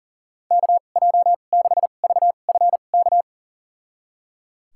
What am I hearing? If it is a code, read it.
Morse code, heard as a keyed tone, about 38 wpm, spaced as for 28 wpm, 700 Hz. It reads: KJ6VFK
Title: USA Call Signs - 6 Characters - 2x Character Spacing - 40wpm